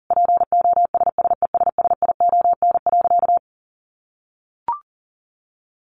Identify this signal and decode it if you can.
Morse code, heard as a keyed tone, about 40 wpm, 700 Hz. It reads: POSSESSION.
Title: Sentences from Top 1400 Words 40wpm